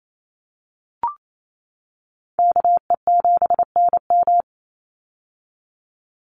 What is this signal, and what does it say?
Morse code, heard as a keyed tone, about 28 wpm, 700 Hz. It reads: KE7NM
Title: USA Call Signs 28wpm